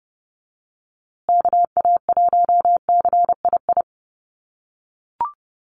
Morse code, heard as a keyed tone, about 30 wpm, 700 Hz. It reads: KA1CII
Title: USA Call Signs Repeated 3x 30wpm